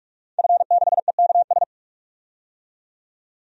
Morse code, heard as a keyed tone, about 42 wpm, 700 Hz. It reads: F6EKS
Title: DXCC-100 Call Signs - 5 Characters - 45wpm